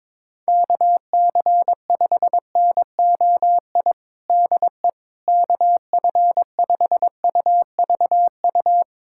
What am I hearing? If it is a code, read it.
Morse code, heard as a keyed tone, about 22 wpm, 700 Hz. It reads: KC5NOI DE KF5UVU
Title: USA Callsign DE Callsign - 6 Characters - 22wpm